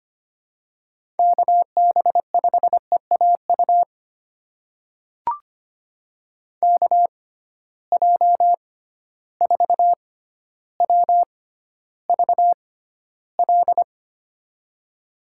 Morse code, heard as a keyed tone, about 25 wpm, spaced as for 13 wpm, 700 Hz. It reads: KB5EAU KJ4WVL
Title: USA Call Signs - 6 Characters - 6x Character Spacing - 25wpm